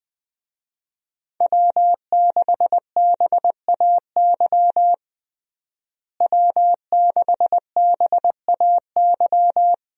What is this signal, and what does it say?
Morse code, heard as a keyed tone, about 20 wpm, 700 Hz. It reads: W6BAY W6BAY
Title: USA Call Signs - 5 Characters - Repeated 2x - 20wpm